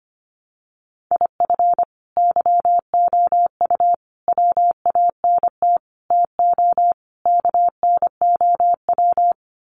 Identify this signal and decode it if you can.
Morse code, heard as a keyed tone, about 25 wpm, 700 Hz. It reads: IF YOU WANT TO KNOW
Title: Sets of 5 Words Encore2 25wpm